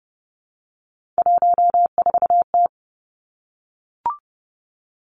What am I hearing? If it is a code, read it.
Morse code, heard as a keyed tone, about 31 wpm, 700 Hz. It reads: 14T
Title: Three Challenging Letter-Number Combinations 30wpm